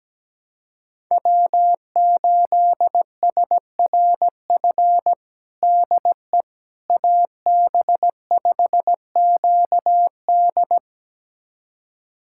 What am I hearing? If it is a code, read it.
Morse code, heard as a keyed tone, about 17 wpm, 700 Hz. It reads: W8SRF DE AB5QD
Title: USA Callsign DE Callsign - 5 Characters - 17wpm